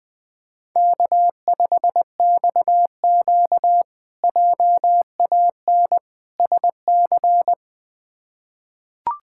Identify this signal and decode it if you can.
Morse code, heard as a keyed tone, about 20 wpm, 700 Hz. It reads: K5XQ JAN SC
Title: K1USN SST QSO Practice - 20wpm